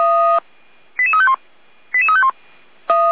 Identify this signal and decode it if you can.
Five-tone selective call (ZVEI), heard as quick runs of five tones, tones of about 70 ms. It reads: 88351 88351